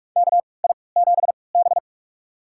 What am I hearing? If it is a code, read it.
Morse code, heard as a keyed tone, about 46 wpm, spaced as for 26 wpm, 690 Hz. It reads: KI7B
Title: USA Call Signs - 4 Characters - 3x Character Spacing - 45wpm